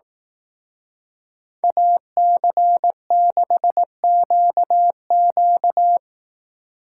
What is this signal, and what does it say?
Morse code, heard as a keyed tone, about 18 wpm, 700 Hz. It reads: AC6QQ
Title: Speed Racing - USA Call Signs 15wpm